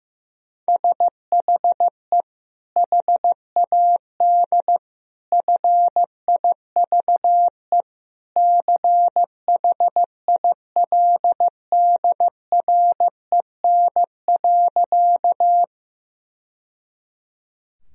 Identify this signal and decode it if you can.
Morse code, heard as a keyed tone, about 14 wpm, 700 Hz. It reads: SHE HAD FIVE CHILDREN.
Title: Sentences from Top 1000 Words - Review Part 2 - 15wpm